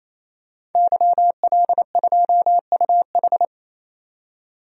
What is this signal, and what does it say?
Morse code, heard as a keyed tone, about 28 wpm, 700 Hz. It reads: YL2UH
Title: DXCC-100 Call Signs - 5 Characters - 28wpm